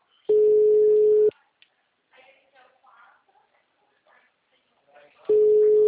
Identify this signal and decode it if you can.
Ringback tone (Europe, 425 Hz), with speech.